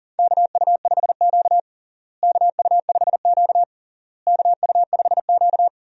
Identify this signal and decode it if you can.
Morse code, heard as a keyed tone, about 40 wpm, 700 Hz. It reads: KU5Q KU5Q KU5Q
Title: USA Call Signs - 4 Characters - Repeated 3x - 40wpm